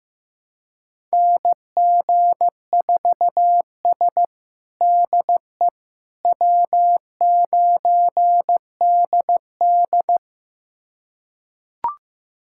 Morse code, heard as a keyed tone, about 15 wpm, 700 Hz. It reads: NG4S DE W9DD
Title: USA Callsign DE Callsign - 4 Characters - 15wpm